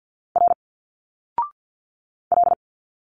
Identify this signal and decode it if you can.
Morse code, heard as a keyed tone, about 50 wpm, 700 Hz. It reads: R L
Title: Single Letters 50wpm